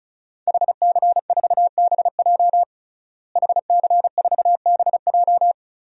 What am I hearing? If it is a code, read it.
Morse code, heard as a keyed tone, about 36 wpm, 700 Hz. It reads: HC4BJ HC4BJ
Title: DXCC-100 Call Signs - 5 Characters - Repeated 2x - 35wpm